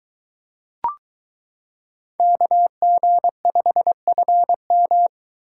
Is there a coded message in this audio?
Yes: Morse code from 2.2 to 5.1 s, heard as a keyed tone, about 22 wpm, 700 Hz. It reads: KG5FM